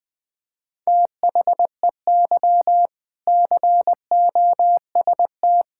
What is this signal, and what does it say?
Morse code, heard as a keyed tone, about 21 wpm, 700 Hz. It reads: THEY COST
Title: Sentences from Top 1000 Words - Review Part 5 - 20wpm